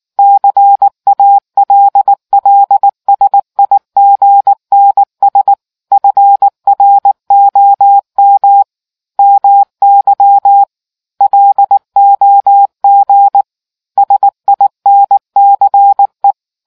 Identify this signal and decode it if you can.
Morse code, heard as a keyed tone, about 19 wpm, 800 Hz. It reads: CALLSIGNS FROM MY LOG SINCE